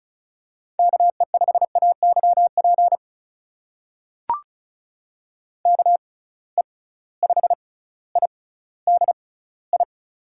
Morse code, heard as a keyed tone, about 36 wpm, spaced as for 18 wpm, 700 Hz. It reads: KE5AYP KE5IDI
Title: USA Call Signs - 6 Characters - 6x Character Spacing - 35wpm